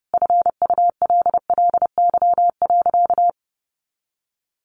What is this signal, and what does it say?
Morse code, heard as a keyed tone, about 30 wpm, 700 Hz. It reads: FULLY.
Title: Sentences from Top 1300 Words 30wpm